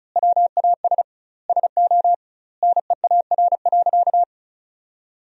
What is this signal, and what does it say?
Morse code, heard as a keyed tone, about 36 wpm, 700 Hz. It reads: WAS SO NEAR.
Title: 2x Word-Spacing - Sentences from Top 300 Words - 35wpm